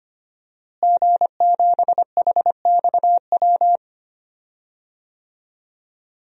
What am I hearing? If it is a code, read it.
Morse code, heard as a keyed tone, about 25 wpm, 700 Hz. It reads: G7HXW
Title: DXCC-100 Call Signs - 5 Characters - Repeated 2x - 25wpm